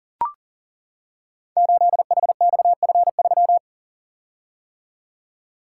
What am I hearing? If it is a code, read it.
Morse code, heard as a keyed tone, about 43 wpm, 700 Hz. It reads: 8HXF3
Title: Code Groups - 5 Letter:Numbers - 40wpm